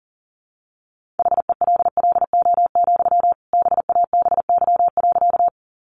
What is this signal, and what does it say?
Morse code, heard as a keyed tone, about 40 wpm, 700 Hz. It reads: HELLO, BABY.
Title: Sentences from Top 1300 Words 40wpm